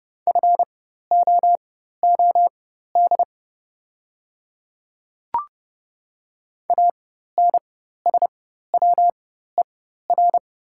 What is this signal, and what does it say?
Morse code, heard as a keyed tone, about 30 wpm, spaced as for 14 wpm, 700 Hz. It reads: FOOD ANSWER
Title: 4x Character Spacing - Top 300 Words - 30wpm